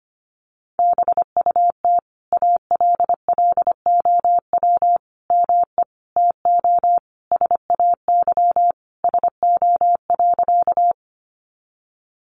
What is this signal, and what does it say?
Morse code, heard as a keyed tone, about 25 wpm, 700 Hz. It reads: BUT ALLOW ME TO SAY SO.
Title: Sentences from Top 700 Words 25wpm